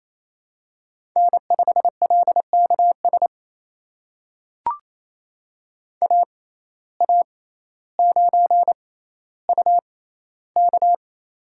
Morse code, heard as a keyed tone, about 27 wpm, spaced as for 14 wpm, 700 Hz. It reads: N5LKS AA9UK
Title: USA Call Signs - 5 Characters - 6x Character Spacing - 28wpm